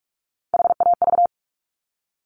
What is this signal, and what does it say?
Morse code, heard as a keyed tone, about 46 wpm, 700 Hz. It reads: HAV
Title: Top Three Letter Combinations in Words 45wpm